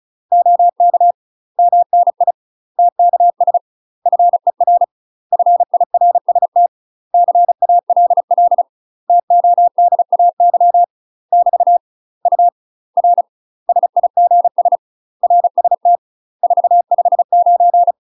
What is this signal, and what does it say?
Morse code, heard as a keyed tone, about 36 wpm, 700 Hz. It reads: OK MNI TKS FER FIRST CALL TODAY <BT> U R SIGS RST 459
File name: UK MASTER QSOs TRAINING / 35WPM / UK MASTER QSOs 0012